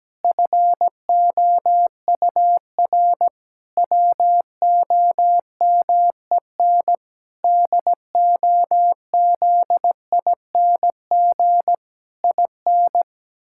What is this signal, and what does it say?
Morse code, heard as a keyed tone, about 17 wpm, 700 Hz. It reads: FOUR WOMEN DOZING IN
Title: Pangram Challenge v2 - 17wpm